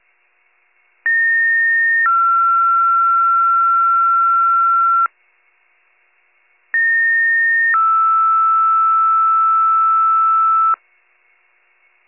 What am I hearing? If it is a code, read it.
Two-tone page, heard as a one-second tone, then a three-second tone, about 1850 then 1360 Hz, about 1850 then 1320 Hz. It is heard from 1.1 to 10.7 s.